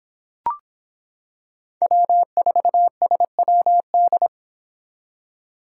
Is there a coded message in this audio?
Yes: Morse code from 1.8 to 4.3 s, heard as a keyed tone, about 25 wpm, 700 Hz. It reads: W4SWD